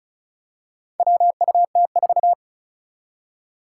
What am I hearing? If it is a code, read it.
Morse code, heard as a keyed tone, about 36 wpm, 700 Hz. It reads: WUT4
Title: Four Challenging Letter-Number Combinations 35wpm